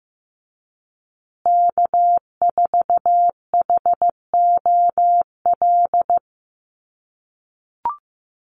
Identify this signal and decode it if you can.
Morse code, heard as a keyed tone, about 15 wpm, 700 Hz. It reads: K4HOL
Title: USA Call Signs Repeated 3x 15wpm